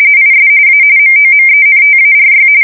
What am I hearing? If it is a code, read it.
RTTY, about 45.45 baud. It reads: IS A MOMENTARY
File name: fsk170 / battleshipnj